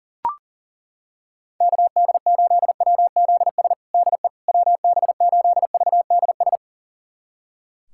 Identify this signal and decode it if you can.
Morse code, heard as a keyed tone, about 40 wpm, 700 Hz. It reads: KD8WZS DE WB8VDS
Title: USA Callsign DE Callsign - 6 Characters - 40wpm